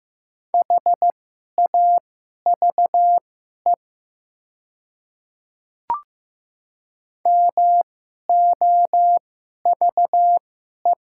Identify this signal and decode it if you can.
Morse code, heard as a keyed tone, about 15 wpm, spaced as for 11 wpm, 700 Hz. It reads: HAVE MOVE
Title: 2x Character Spacing - Top 200 Words - 15wpm